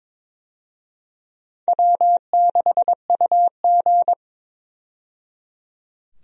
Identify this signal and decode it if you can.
Morse code, heard as a keyed tone, about 22 wpm, 700 Hz. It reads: W6UG
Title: USA Call Signs - 4 Characters - 22wpm